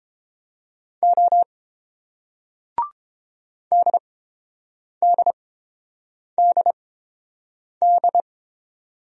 Morse code, heard as a keyed tone, about 29 wpm, spaced as for 7 wpm, 690 Hz. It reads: O DDDD